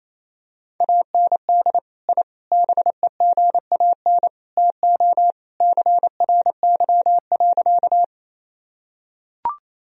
Morse code, heard as a keyed tone, about 28 wpm, 700 Hz. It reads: AND I BEGAN TO CRY.
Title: Sentences from Top 1000 Words - Review Part 2 - 28wpm